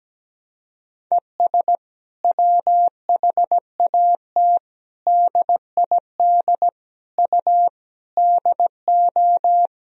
Morse code, heard as a keyed tone, about 17 wpm, 700 Hz. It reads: ES WHAT DID U DO
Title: Sentences from Top 100 Words and Abbreviations 17wpm